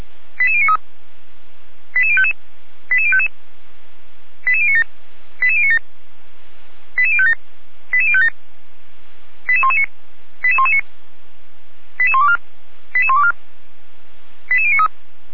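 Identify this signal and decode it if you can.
Five-tone selective call (ZVEI), heard as quick runs of five tones, tones of about 70 ms. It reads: 88093 88055 88055 88097 88097 88057 88057 88119 88119 88124 88124 88093
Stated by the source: ZVEI five-tone call received on SDR.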